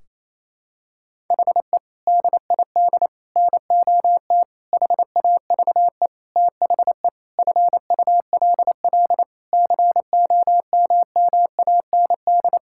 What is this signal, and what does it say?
Morse code, heard as a keyed tone, about 28 wpm, 700 Hz. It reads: HE DID NOT HAVE THE FULL COMMAND
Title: Sentences from Top 1600 Words 28wpm